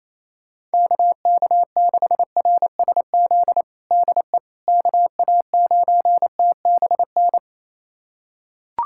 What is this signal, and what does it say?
Morse code, heard as a keyed tone, about 28 wpm, 700 Hz. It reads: KK6RSZ DE KA9TBN
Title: USA Callsign DE Callsign 28wpm